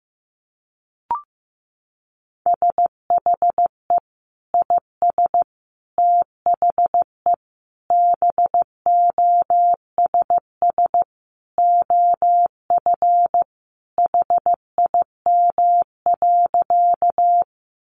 Morse code, heard as a keyed tone, about 15 wpm, 700 Hz. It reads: SHE IS THE BOSS OF HIM.
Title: Sentences from Top 700 Words 15wpm